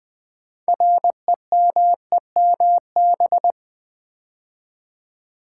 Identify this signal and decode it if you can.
Morse code, heard as a keyed tone, about 20 wpm, 700 Hz. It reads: REMEMB
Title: Build A Word - 2 Letters At A Time - 8-Letter Words - 20wpm